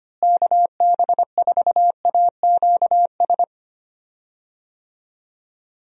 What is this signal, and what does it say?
Morse code, heard as a keyed tone, about 25 wpm, 700 Hz. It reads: KB4AQS
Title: USA Call Signs - 6 Characters - Repeated 2x - 25wpm